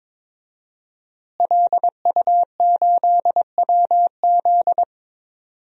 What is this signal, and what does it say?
Morse code, heard as a keyed tone, about 23 wpm, 700 Hz. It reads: LU8WZ
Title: DXCC-100 Call Signs - 5 Characters - Repeated 3x - 22wpm